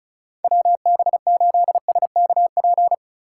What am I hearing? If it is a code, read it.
Morse code, heard as a keyed tone, about 36 wpm, 700 Hz. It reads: WB8SKP
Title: K1USN SST QSO Practice - 35wpm